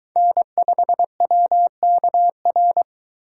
Morse code, heard as a keyed tone, about 21 wpm, 700 Hz. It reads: N5WKR